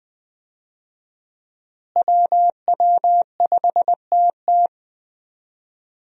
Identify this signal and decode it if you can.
Morse code, heard as a keyed tone, about 20 wpm, 700 Hz. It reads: WW5TT
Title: USA Call Signs - 5 Characters - Repeated 3x - 20wpm